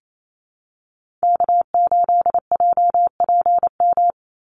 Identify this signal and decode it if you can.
Morse code, heard as a keyed tone, about 28 wpm, 700 Hz. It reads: K8JPM
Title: Call Signs - 28wpm